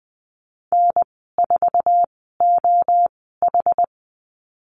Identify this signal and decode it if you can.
Morse code, heard as a keyed tone, about 20 wpm, spaced as for 14 wpm, 700 Hz. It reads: N4OH
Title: 082 Lesson 14g - Number 2 - Call Signs 20wpm